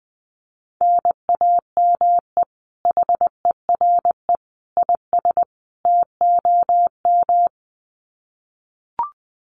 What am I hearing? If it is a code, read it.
Morse code, heard as a keyed tone, about 20 wpm, 700 Hz. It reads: NAME HERE IS TOM
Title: Common QSO Phrases 20wpm